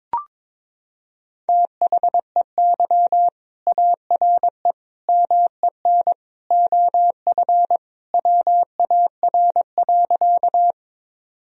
Morse code, heard as a keyed tone, about 22 wpm, 700 Hz. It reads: THEY ARE MEN OF WAR.
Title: Sentences from Top 1000 Words - Review Part 2 - 22wpm